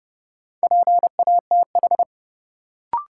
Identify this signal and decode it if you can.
Morse code, heard as a keyed tone, about 32 wpm, 700 Hz. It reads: PATH
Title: Sentences from Top 1400 Words - New Word Drill Alt - 30wpm